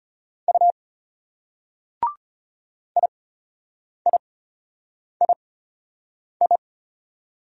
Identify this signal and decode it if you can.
Morse code, heard as a keyed tone, about 33 wpm, spaced as for 8 wpm, 700 Hz. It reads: U IIII